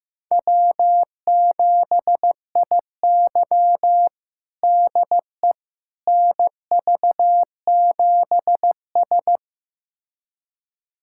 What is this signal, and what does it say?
Morse code, heard as a keyed tone, about 15 wpm, 700 Hz. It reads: W7IY DE NV7S
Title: USA Callsign DE Callsign - 4 Characters - 15wpm